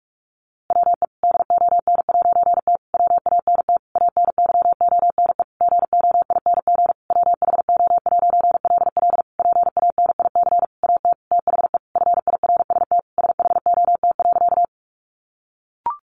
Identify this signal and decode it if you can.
Morse code, heard as a keyed tone, about 44 wpm, 700 Hz. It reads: WE DON'T WANT ANYONE GOING WHO'LL PANIC AT THE FIRST SHOT.
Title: A Sound Of Thunder 45wpm